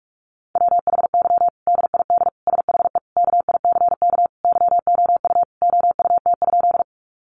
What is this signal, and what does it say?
Morse code, heard as a keyed tone, about 44 wpm, 700 Hz. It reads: WHY DID SHE KICK YOU OUT?
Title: Sentences from Top 1300 Words 45wpm